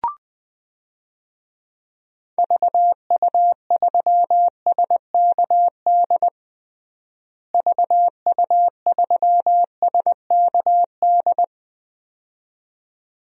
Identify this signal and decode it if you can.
Morse code, heard as a keyed tone, about 21 wpm, 700 Hz. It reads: VU3SKD VU3SKD
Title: DXCC-100 Call Signs - 6 Characters - Repeated 2x - 20wpm